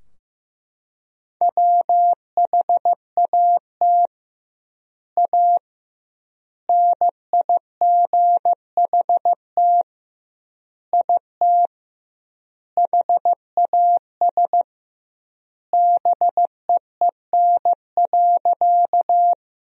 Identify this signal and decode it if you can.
Morse code, heard as a keyed tone, about 15 wpm, 700 Hz. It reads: WHAT A NIGHT IT HAS BEEN.
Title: 2x Word-Spacing - Sentences from Top 300 Words - 15wpm